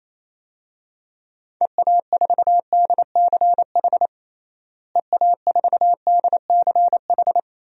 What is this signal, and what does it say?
Morse code, heard as a keyed tone, about 28 wpm, 700 Hz. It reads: EA4DCH EA4DCH
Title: DXCC-100 Call Signs - 6 Characters - Repeated 2x - 28wpm